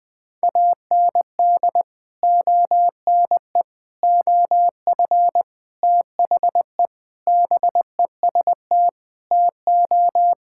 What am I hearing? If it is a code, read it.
Morse code, heard as a keyed tone, about 20 wpm, 700 Hz. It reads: AND ONE OF THE BEST TO
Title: Sentences with 4-Word Phrases - 20wpm